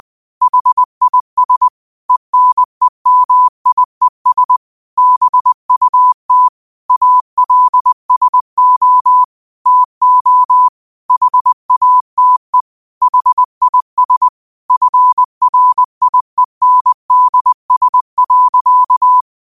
Morse code, heard as a keyed tone, about 20 wpm, 1000 Hz. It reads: HIS ENEMIES BUT ALSO TO HATE HIS FRIENDS.
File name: qotdfiles / 20wpm